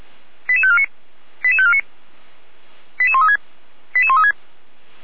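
Five-tone selective call (ZVEI), heard as quick runs of five tones, tones of about 70 ms. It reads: 88549 88549 88136 88136
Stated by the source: ZVEI five-tone call received by SDR, from feuerwehr-orlamuende.org (fire brigade Orlamünde).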